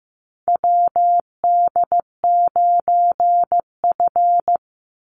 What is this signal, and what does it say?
Morse code, heard as a keyed tone, about 15 wpm, 700 Hz. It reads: WD9F
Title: NA SOTA Chaser Callsigns 15wpm